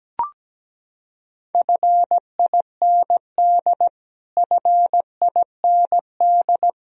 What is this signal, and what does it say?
Morse code, heard as a keyed tone, about 17 wpm, 700 Hz. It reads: FIND FIND
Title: Top 100 Words Repeated 17wpm